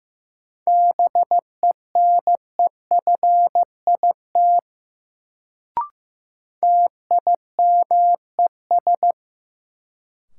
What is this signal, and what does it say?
Morse code, heard as a keyed tone, about 15 wpm, 700 Hz. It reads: BENEFIT TIMES
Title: Sentences from Top 1000 Words - Review Part 6 - New Word Drill - 15wpm